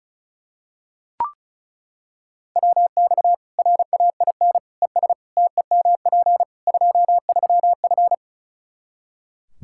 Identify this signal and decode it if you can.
Morse code, heard as a keyed tone, about 34 wpm, 700 Hz. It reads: WX RAIN ES TEMP 23F
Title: K1USN SST QSO Practice 35wpm